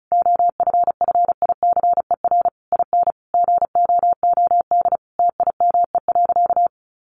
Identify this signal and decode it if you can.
Morse code, heard as a keyed tone, about 34 wpm, 700 Hz. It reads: OFFICER IN GOOD TIME.
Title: Sentences from Top 1200 Words 35wpm